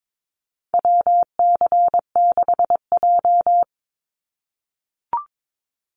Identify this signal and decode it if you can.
Morse code, heard as a keyed tone, about 22 wpm, 700 Hz. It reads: WC6J
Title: NA SOTA Activator Callsigns 22wpm